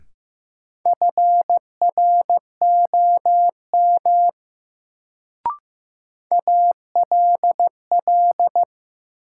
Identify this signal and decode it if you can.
Morse code, heard as a keyed tone, about 15 wpm, 700 Hz. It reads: FROM ALL
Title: Sentences from Top 100 Words - New Word Drill - 15wpm